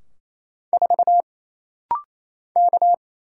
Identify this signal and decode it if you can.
Morse code, heard as a keyed tone, about 28 wpm, 700 Hz. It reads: 4 K
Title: Single Letter-Number - Rapid-Fire 28wpm